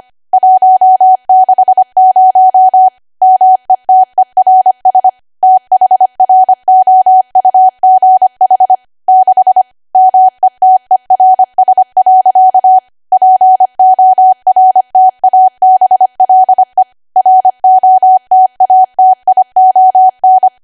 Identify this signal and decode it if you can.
Morse code, heard as a keyed tone, about 25 wpm, 750 Hz. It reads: 160 METERS THROUGH 6 METERS. PORTABLE ROTATION
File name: Archive / 25 WPM / 25WPM